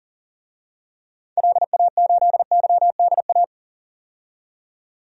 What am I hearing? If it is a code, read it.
Morse code, heard as a keyed tone, about 40 wpm, 700 Hz. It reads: LA8YDA
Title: DXCC-100 Call Signs - 6 Characters - 40wpm